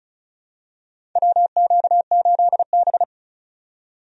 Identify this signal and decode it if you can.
Morse code, heard as a keyed tone, about 34 wpm, 700 Hz. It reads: WQ8B
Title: USA Call Signs - 4 Characters - 35wpm